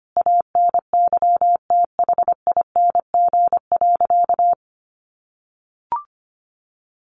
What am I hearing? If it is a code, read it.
Morse code, heard as a keyed tone, about 25 wpm, 700 Hz. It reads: ANYTHING.
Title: A Sound Of Thunder 25wpm